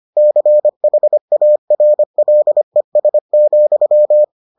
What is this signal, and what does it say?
Morse code, heard as a keyed tone, about 25 wpm, 600 Hz. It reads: CHARLES,